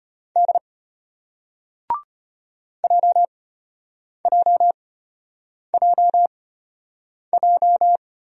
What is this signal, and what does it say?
Morse code, heard as a keyed tone, about 33 wpm, spaced as for 8 wpm, 700 Hz. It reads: D JJJJ